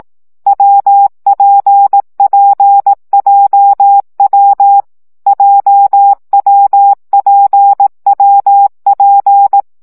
Morse code, heard as a keyed tone, about 18 wpm, 800 Hz. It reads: WPPJW JWPWP
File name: CW gyakorlas w p j 15 baud